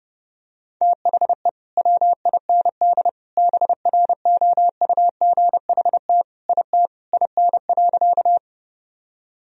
Morse code, heard as a keyed tone, about 30 wpm, 700 Hz. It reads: THE WIND BROUGHT IT IN.
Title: Sentences from Top 1000 Words - Review Part 2 - 30wpm